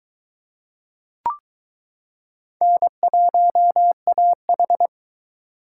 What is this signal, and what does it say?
Morse code, heard as a keyed tone, about 22 wpm, 700 Hz. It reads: N1AH